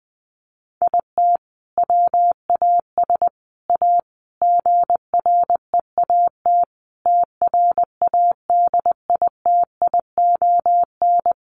Morse code, heard as a keyed tone, about 20 wpm, 700 Hz. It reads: IT WAS A GREAT TRADITION
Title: Sentences from Top 1500 Words 20wpm